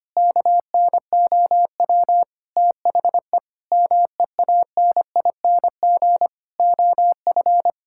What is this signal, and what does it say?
Morse code, heard as a keyed tone, about 25 wpm, 700 Hz. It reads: KNOW THE MEANING OF
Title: Sentences from Top 1000 Words - Review Part 6 - 25wpm